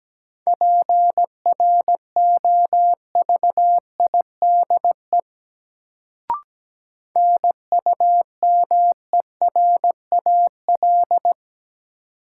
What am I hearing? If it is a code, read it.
Morse code, heard as a keyed tone, about 17 wpm, 700 Hz. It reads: PROVIDE NUMERAL
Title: Sentences from Top 1000 Words - Review Part 2 - New Word Drill - 17wpm